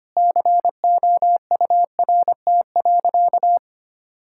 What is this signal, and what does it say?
Morse code, heard as a keyed tone, about 25 wpm, 700 Hz. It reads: COURT.
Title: Sentences from Top 1600 Words 25wpm